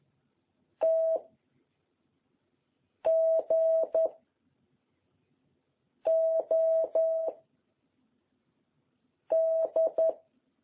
Morse code, heard as a keyed tone, about 11 wpm, 660 Hz. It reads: T G O D
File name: jh8cbh / 2021cw